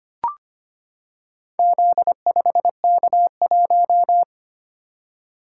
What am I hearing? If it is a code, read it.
Morse code, heard as a keyed tone, about 25 wpm, 690 Hz. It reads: Z5K1
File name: practice / 25wpm / Four Letter-Number Combinations - 25wpm